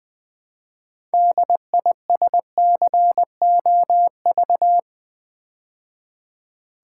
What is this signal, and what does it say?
Morse code, heard as a keyed tone, about 20 wpm, 700 Hz. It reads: DISCOV
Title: Build A Word - 2 Letters At A Time - 8-Letter Words - 20wpm